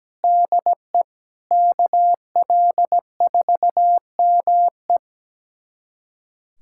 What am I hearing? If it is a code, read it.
Morse code, heard as a keyed tone, about 17 wpm, 700 Hz. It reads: DE KL4ME
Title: USA Callsign DE Callsign 17wpm